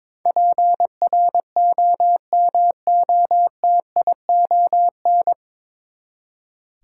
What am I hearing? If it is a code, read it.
Morse code, heard as a keyed tone, about 22 wpm, 700 Hz. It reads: PROMOTION
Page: Morse Code Ninja practice 22wpm